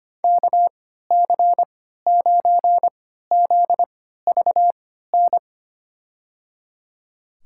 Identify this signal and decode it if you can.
Morse code, heard as a keyed tone, about 25 wpm, spaced as for 14 wpm, 700 Hz. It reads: KC9ZVN
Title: USA Call Signs - 6 Characters - 3x Character Spacing - 25wpm